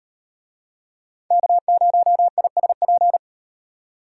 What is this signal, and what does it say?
Morse code, heard as a keyed tone, about 36 wpm, 700 Hz. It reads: K0ISP